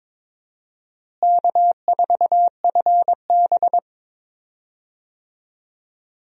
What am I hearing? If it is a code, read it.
Morse code, heard as a keyed tone, about 22 wpm, 700 Hz. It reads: K4FB
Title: USA Call Signs - 4 Characters - Repeated 3x - 22wpm